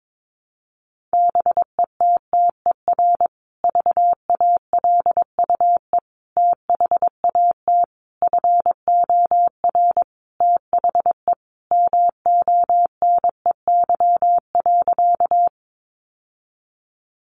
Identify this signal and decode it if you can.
Morse code, heard as a keyed tone, about 22 wpm, 700 Hz. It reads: BETTER VALUE THAT FOR THE MONEY.
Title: Sentences from Top 600 Words 22wpm